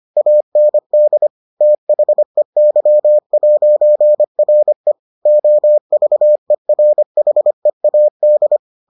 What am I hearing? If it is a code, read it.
Morse code, heard as a keyed tone, about 25 wpm, 600 Hz. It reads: AND THEY'RE OVERHEAD